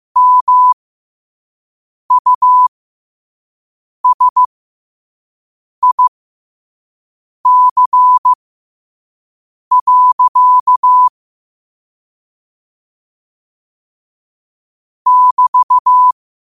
Morse code, heard as a keyed tone, about 15 wpm, spaced as for 5 wpm, 1000 Hz. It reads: MUSIC. <BT>
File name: qotdfiles / 5wpm